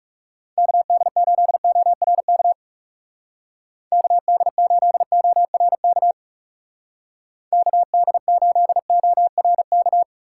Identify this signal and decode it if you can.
Morse code, heard as a keyed tone, about 38 wpm, 700 Hz. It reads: KD8ORK KD8ORK KD8ORK